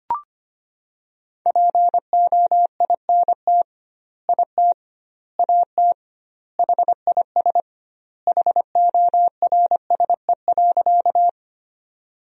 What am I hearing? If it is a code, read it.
Morse code, heard as a keyed tone, about 25 wpm, 700 Hz. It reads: POINT IT AT HIS HORSE.
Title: 2x Word-Spacing - Sentences from Top 300 Words - 25wpm